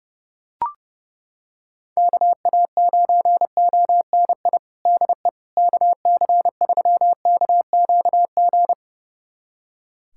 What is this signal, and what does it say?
Morse code, heard as a keyed tone, about 30 wpm, 700 Hz. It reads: KA9ONI DE KC3KQG
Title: USA Callsign DE Callsign - 6 Characters - 30wpm